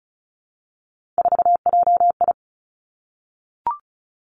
Morse code, heard as a keyed tone, about 36 wpm, 700 Hz. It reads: 4JI
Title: Three Challenging Letter-Number Combinations 35wpm